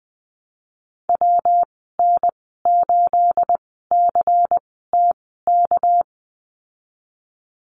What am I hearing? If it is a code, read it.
Morse code, heard as a keyed tone, about 20 wpm, spaced as for 15 wpm, 700 Hz. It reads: WN8CTK
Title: 219 Lesson 35g - Number 0 - Call Signs 20wpm